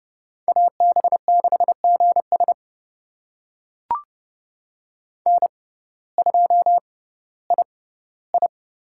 Morse code, heard as a keyed tone, about 30 wpm, spaced as for 17 wpm, 700 Hz. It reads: AB6GS N2II
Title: USA Call Signs - 5 Characters - 6x Character Spacing - 30wpm